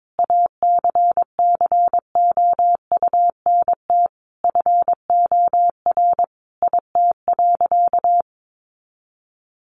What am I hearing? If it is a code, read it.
Morse code, heard as a keyed tone, about 22 wpm, 700 Hz. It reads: ACCOUNT FOR IT.
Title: Sentences from Top 900 Words 22wpm